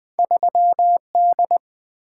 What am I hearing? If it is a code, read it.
Morse code, heard as a keyed tone, about 20 wpm, 700 Hz. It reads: 3D
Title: Two Letter-Number Combinations - 20wpm